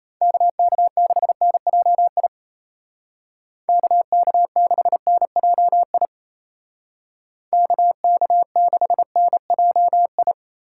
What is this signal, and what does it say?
Morse code, heard as a keyed tone, about 33 wpm, 700 Hz. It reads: KK6NJI KK6NJI KK6NJI
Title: Speed Racing - USA 2x3 Call Signs 28wpm